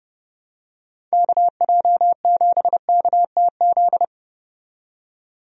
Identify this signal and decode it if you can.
Morse code, heard as a keyed tone, about 29 wpm, 700 Hz. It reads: KJ7KTZ
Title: USA Call Signs - 6 Characters - 30wpm